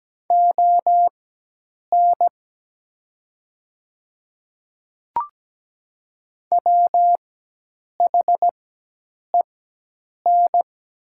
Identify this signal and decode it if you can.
Morse code, heard as a keyed tone, about 17 wpm, spaced as for 8 wpm, 700 Hz. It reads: ON WHEN
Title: 4x Character Spacing - Top 100 Words - 17wpm